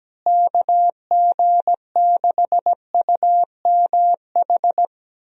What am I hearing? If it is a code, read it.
Morse code, heard as a keyed tone, about 17 wpm, 700 Hz. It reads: KG6UMH
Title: USA Call Signs - 6 Characters - Repeated 2x - 17wpm